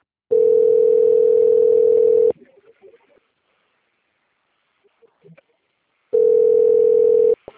Ringback tone (North America), 440 + 480 Hz.